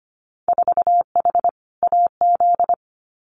Morse code, heard as a keyed tone, about 25 wpm, 700 Hz. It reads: 4H AZ
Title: Winter Field Day Exchanges Encore 25wpm